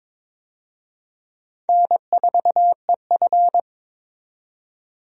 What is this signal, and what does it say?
Morse code, heard as a keyed tone, about 23 wpm, 700 Hz. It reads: N4EF
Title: USA Call Signs - 4 Characters - 22wpm